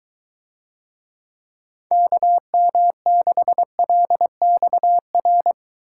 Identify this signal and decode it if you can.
Morse code, heard as a keyed tone, about 23 wpm, 700 Hz. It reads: KM6LXR